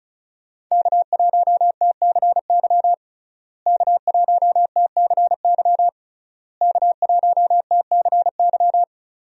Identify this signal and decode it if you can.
Morse code, heard as a keyed tone, about 36 wpm, 700 Hz. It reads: K1TCY K1TCY K1TCY
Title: USA Call Signs - 5 Characters - Repeated 3x - 35wpm